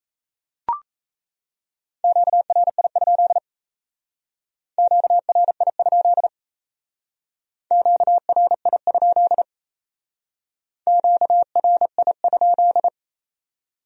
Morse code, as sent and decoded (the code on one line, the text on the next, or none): --.- .-. .. ..--.. / --.- .-. .. ..--.. / --.- .-. .. ..--.. / --.- .-. .. ..--..
QRI? QRI? QRI? QRI?